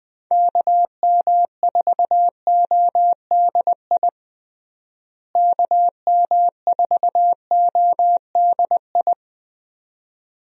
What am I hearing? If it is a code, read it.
Morse code, heard as a keyed tone, about 20 wpm, 700 Hz. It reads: KM4ODI KM4ODI
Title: USA Call Signs - 6 Characters - Repeated 2x - 20wpm